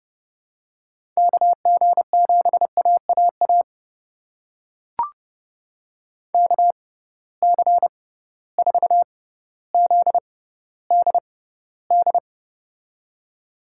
Morse code, heard as a keyed tone, about 32 wpm, spaced as for 16 wpm, 700 Hz. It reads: KG7AAA KC4ZDD
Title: USA Call Signs - 6 Characters - 6x Character Spacing - 30wpm